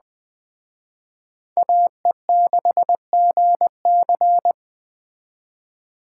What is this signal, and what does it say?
Morse code, heard as a keyed tone, about 20 wpm, 700 Hz. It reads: AE6GC